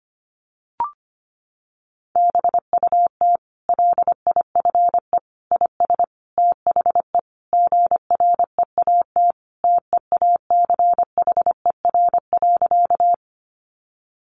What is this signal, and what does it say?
Morse code, heard as a keyed tone, about 25 wpm, 700 Hz. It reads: BUT LIFE IS THE GREAT TEACHER.
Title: Sentences from Top 900 Words 25wpm